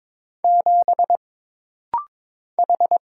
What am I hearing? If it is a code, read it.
Morse code, heard as a keyed tone, about 22 wpm, 700 Hz. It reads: 7 H
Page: This rapid-fire timing is geared to push your ICR (Instant Character Recognition) skills.